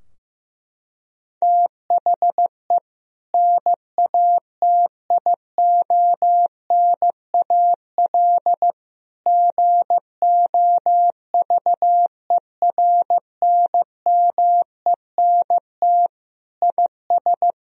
Morse code, heard as a keyed tone, about 15 wpm, 700 Hz. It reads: THE NATIONAL GOVERNMENT IS
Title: Sentences from Top 1000 Words - Review Part 4 - 15wpm